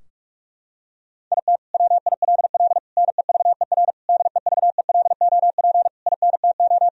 Morse code, heard as a keyed tone, about 46 wpm, 700 Hz. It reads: IT WILL NEVER DEVELOP INTO
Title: Sentences from Top 1000 Words - Review Part 2 - 45wpm